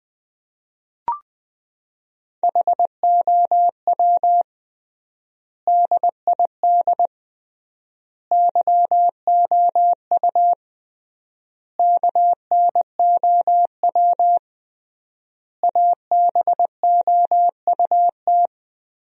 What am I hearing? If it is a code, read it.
Morse code, heard as a keyed tone, about 21 wpm, 700 Hz. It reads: HOW DID YOU KNOW ABOUT
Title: 3x Word-Spacing - Sentences from Top 100 Words - 20wpm